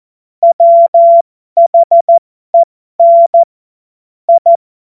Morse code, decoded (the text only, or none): WHEN I